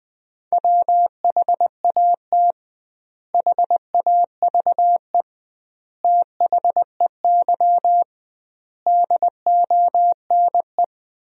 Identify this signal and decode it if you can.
Morse code, heard as a keyed tone, about 20 wpm, 700 Hz. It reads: WHAT HAVE THEY DONE
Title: 2x Word-Spacing - Sentences from Top 500 Words - 20wpm